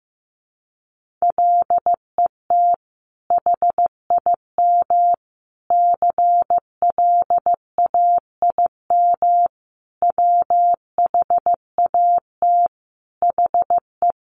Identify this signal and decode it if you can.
Morse code, heard as a keyed tone, about 15 wpm, 700 Hz. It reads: LET HIM CLAIM WHAT HE
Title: Sentences from Top 1500 Words 15wpm